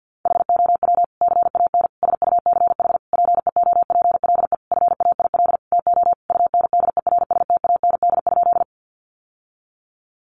Morse code, heard as a keyed tone, about 50 wpm, 700 Hz. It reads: HOW CAN SUCH PEOPLE FAIL TO UNDERSTAND?
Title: Sentences from Top 1100 Words 50wpm